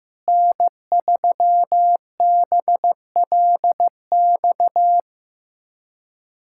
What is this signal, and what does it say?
Morse code, heard as a keyed tone, about 15 wpm, 700 Hz. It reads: N3BLX
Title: USA Callsign DE Callsign - 5 Characters - 15wpm